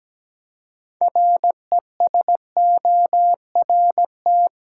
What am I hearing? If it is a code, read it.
Morse code, heard as a keyed tone, about 17 wpm, 700 Hz. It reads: RESORT
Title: Six Letter Words - 17wpm